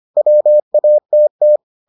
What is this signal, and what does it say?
Morse code, heard as a keyed tone, about 25 wpm, 600 Hz. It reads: WATT